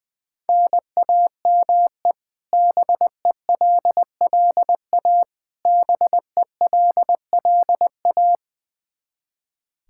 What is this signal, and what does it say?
Morse code, heard as a keyed tone, about 20 wpm, 700 Hz. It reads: NAME BELLA BELLA
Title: Practice QSO - Exchange 1 - Beginning - 20wpm